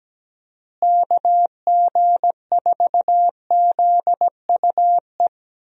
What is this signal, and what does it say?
Morse code, heard as a keyed tone, about 17 wpm, 700 Hz. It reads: KG4ZUE
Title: USA Call Signs - 6 Characters - Repeated 2x - 17wpm